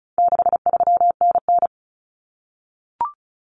Morse code, heard as a keyed tone, about 34 wpm, 700 Hz. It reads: 63NN
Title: Common Cut Numbers - 4-Digit 35wpm